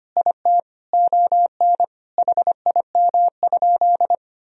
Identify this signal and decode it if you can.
Morse code, heard as a keyed tone, about 25 wpm, 700 Hz. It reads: IT ON HIM?
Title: Sentences from Top 1000 Words - Review Part 1 - 25wpm